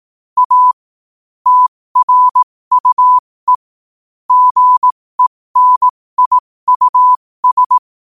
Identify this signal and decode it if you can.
Morse code, heard as a keyed tone, about 18 wpm, spaced as for 15 wpm, 1000 Hz. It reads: A TRUE GENIUS
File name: qotdfiles / 15wpm